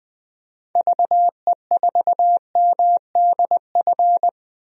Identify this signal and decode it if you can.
Morse code, heard as a keyed tone, about 20 wpm, 700 Hz. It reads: VE4MDF
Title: DXCC-100 Call Signs - 6 Characters - 20wpm